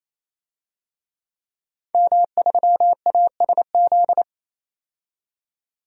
Morse code, heard as a keyed tone, about 28 wpm, 700 Hz. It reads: M3ASZ
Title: DXCC-100 Call Signs - 5 Characters - 28wpm